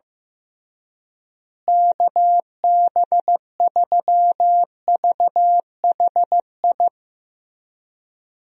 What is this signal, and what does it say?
Morse code, heard as a keyed tone, about 15 wpm, 700 Hz. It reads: KB3VHI
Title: USA Call Signs - 6 Characters - Repeated 2x - 15wpm